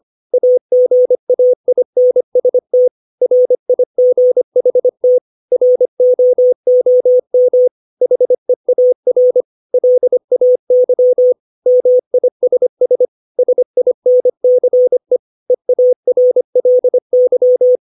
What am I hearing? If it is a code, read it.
Morse code, heard as a keyed tone, about 25 wpm, 500 Hz. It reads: AGAINST RIGHT ROOM HEAR LAY MISS SINCE EARLY
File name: 632 words / 632 Words 25WPM